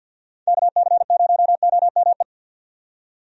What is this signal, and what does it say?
Morse code, heard as a keyed tone, about 48 wpm, 700 Hz. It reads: KC0OME
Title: USA Call Signs - 6 Characters - 50wpm